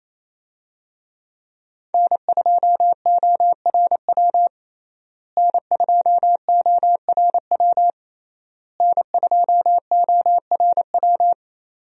Morse code, heard as a keyed tone, about 28 wpm, 700 Hz. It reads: N2ORW N2ORW N2ORW
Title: USA Call Signs - 5 Characters - Repeated 3x - 28wpm